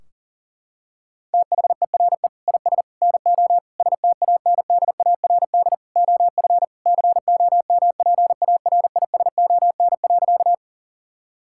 Morse code, heard as a keyed tone, about 40 wpm, 700 Hz. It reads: THERE IS NO STANDARD OF COMPARISON.
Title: Sentences from Top 1600 Words 40wpm